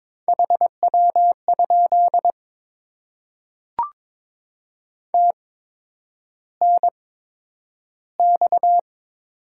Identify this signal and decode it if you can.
Morse code, heard as a keyed tone, about 22 wpm, spaced as for 9 wpm, 700 Hz. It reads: HW? TNX
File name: Top 20 QSO Elements - 8x Character Spacing - 22wpm